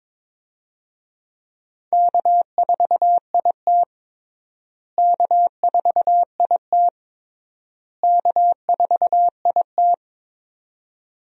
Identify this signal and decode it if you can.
Morse code, heard as a keyed tone, about 22 wpm, 700 Hz. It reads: K4IT K4IT K4IT
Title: USA Call Signs - 4 Characters - Repeated 3x - 22wpm